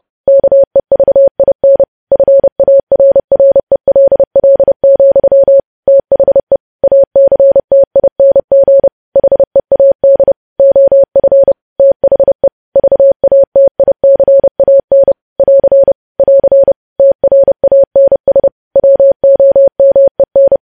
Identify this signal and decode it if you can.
Morse code, heard as a keyed tone, about 30 wpm, 550 Hz. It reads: KEVIN FARRELL, THE ACTING HEAD OF THE VATICAN <AR> <AR> TRANS WOMEN
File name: podcast / News headlines-1x-30x30